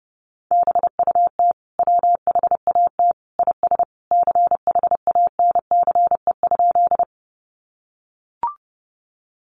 Morse code, heard as a keyed tone, about 30 wpm, 700 Hz. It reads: BUT WHAT IS CHANCE?
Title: Sentences from Top 800 Words 30wpm